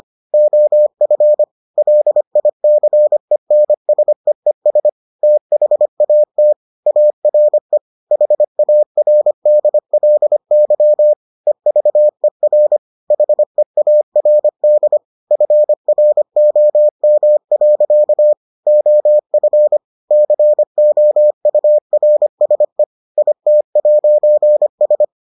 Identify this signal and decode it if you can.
Morse code, heard as a keyed tone, about 25 wpm, 600 Hz. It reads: OF LICENSEES THAT ARE HARDLY EVER HEARD FROM. OF COURSE IT'S